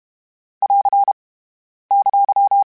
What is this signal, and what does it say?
Morse code, heard as a keyed tone, about 32 wpm, 800 Hz. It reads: <AR> !
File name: (spaced) / Q-Calls Spaced-mp3s / Q Calls Twice sp @30